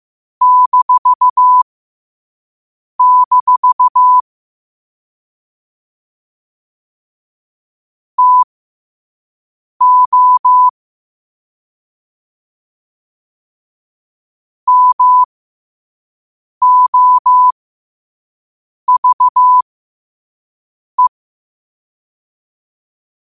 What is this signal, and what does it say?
Morse code, heard as a keyed tone, about 15 wpm, spaced as for 5 wpm, 1000 Hz. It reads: -- TO MOVE
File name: qotdfiles / 5wpm